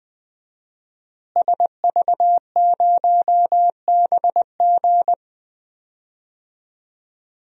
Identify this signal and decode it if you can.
Morse code, heard as a keyed tone, about 20 wpm, 700 Hz. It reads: SV0BG
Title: DXCC-100 Call Signs - 5 Characters - Repeated 3x - 20wpm